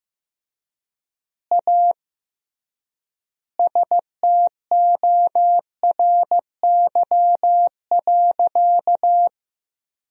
Morse code, heard as a keyed tone, about 15 wpm, 700 Hz. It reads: A STORY.
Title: 3x Word-Spacing - Sentences from Top 300 Words - 15wpm